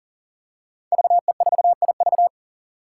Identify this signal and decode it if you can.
Morse code, heard as a keyed tone, about 40 wpm, 700 Hz. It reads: VE4IV
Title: Five Challenging Letter-Number Combinations 40wpm